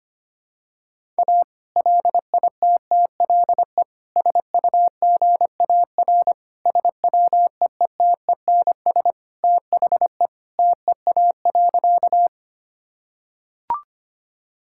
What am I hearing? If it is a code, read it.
Morse code, heard as a keyed tone, about 25 wpm, 700 Hz. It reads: A LITTLE SUGAR SWEETENS THE TEA.
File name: practice / 25wpm / Sentences with 2-Word Phrases - 25wpm